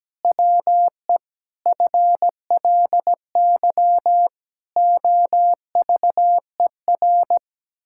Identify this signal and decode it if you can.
Morse code, heard as a keyed tone, about 17 wpm, 700 Hz. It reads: WE FLY OVER
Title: Pangram Challenge v2 - 17wpm